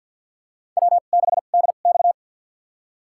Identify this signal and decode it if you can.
Morse code, heard as a keyed tone, about 53 wpm, spaced as for 37 wpm, 700 Hz. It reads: W6DX
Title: USA Call Signs - 4 Characters - 2x Character Spacing - 50wpm